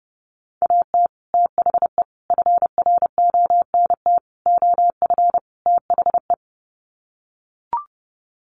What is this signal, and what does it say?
Morse code, heard as a keyed tone, about 30 wpm, 700 Hz. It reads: AT THE FRONT OF THE
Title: Sets of 5 Words 30wpm